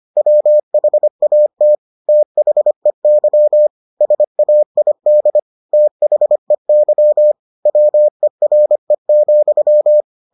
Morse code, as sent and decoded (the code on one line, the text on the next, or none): .-- .... .- - / - .... -.-- / ... .- .. -.. / - .... -.-- / .-- . .-. --..--
WHAT THEY SAID THEY WERE,